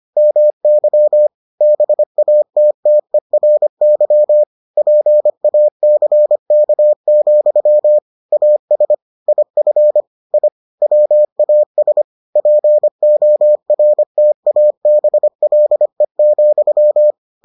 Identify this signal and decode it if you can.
Morse code, heard as a keyed tone, about 25 wpm, 600 Hz. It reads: MY BATTERY PACK, AS IF I WAS PORTABLE,